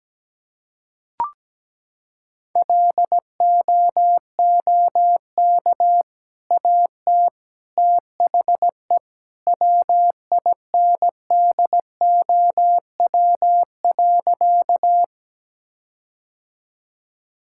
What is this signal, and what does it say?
Morse code, heard as a keyed tone, about 17 wpm, 700 Hz. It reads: LOOK AT THE WINDOW.
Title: Sentences from Top 1600 Words 17wpm